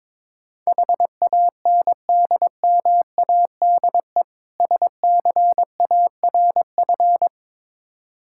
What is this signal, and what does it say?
Morse code, heard as a keyed tone, about 22 wpm, 690 Hz. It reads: HANDMADE SCARF
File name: Difficult Words with Hints - 22wpm